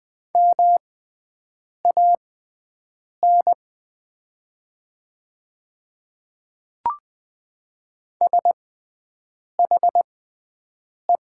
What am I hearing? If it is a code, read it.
Morse code, heard as a keyed tone, about 20 wpm, spaced as for 7 wpm, 700 Hz. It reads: MAN SHE
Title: 6x Character Spacing - Top 200 Words - 20wpm